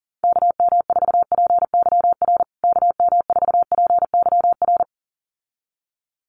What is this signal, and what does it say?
Morse code, heard as a keyed tone, about 40 wpm, 700 Hz. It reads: KM4PYR KM4PYR
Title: USA Call Signs Repeated 2x 40wpm